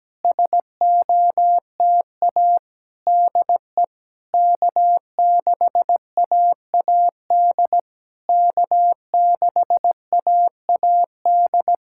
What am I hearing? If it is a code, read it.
Morse code, heard as a keyed tone, about 16 wpm, 700 Hz. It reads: SOTA DE K6AAD K6AAD
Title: SOTA QSO Practice 17wpm